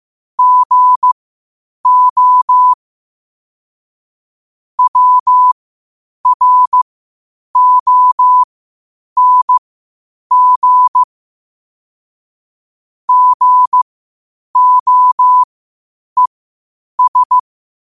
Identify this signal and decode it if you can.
Morse code, heard as a keyed tone, about 15 wpm, spaced as for 8 wpm, 1000 Hz. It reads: GO WRONG GOES
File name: qotdfiles / 8wpm